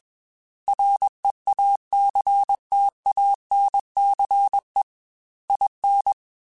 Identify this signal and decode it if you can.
Morse code, heard as a keyed tone, about 21 wpm, 800 Hz. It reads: REACTANCE IN